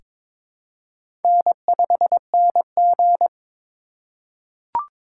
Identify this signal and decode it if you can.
Morse code, heard as a keyed tone, about 22 wpm, 700 Hz. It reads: N5NG
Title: USA Call Signs - 4 Characters - 4x Character Spacing - 22wpm